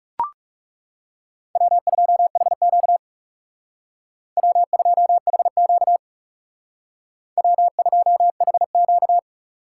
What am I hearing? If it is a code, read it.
Morse code, heard as a keyed tone, about 39 wpm, 700 Hz. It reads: W2HQ W2HQ W2HQ